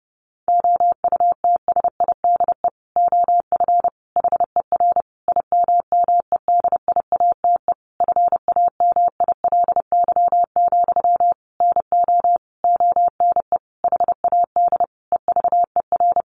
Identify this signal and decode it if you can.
Morse code, heard as a keyed tone, about 30 wpm, 700 Hz. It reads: OUTSIDE OF HER IMMEDIATE FAMILY, NO ONE HAD EVER
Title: Sentences from Top 1200 Words 30wpm